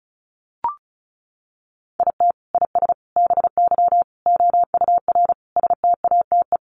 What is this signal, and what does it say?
Morse code, heard as a keyed tone, about 34 wpm, 700 Hz. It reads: IT IS BY OUR STATE
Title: Sentences from Top 1400 Words 35wpm